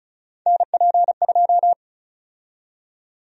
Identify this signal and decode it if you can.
Morse code, heard as a keyed tone, about 36 wpm, 700 Hz. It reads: NP2
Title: USA Call Sign Prefixes - 35wpm